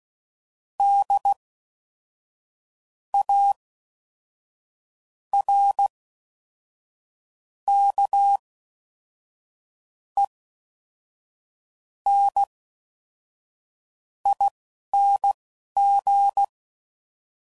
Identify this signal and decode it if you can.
Morse code, heard as a keyed tone, about 16 wpm, spaced as for 5 wpm, 800 Hz. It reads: DARKENING